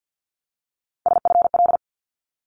Morse code, heard as a keyed tone, about 50 wpm, 700 Hz. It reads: SFL
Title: ARRL Sections 50wpm